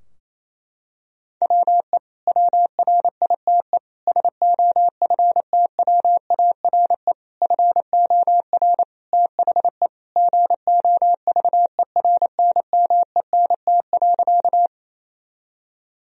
Morse code, heard as a keyed tone, about 27 wpm, 700 Hz. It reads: WE WRITE SOFTWARE FOR THE GOVERNMENT.
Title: Sentences from Top 1000 Words - Review Part 3 - 28wpm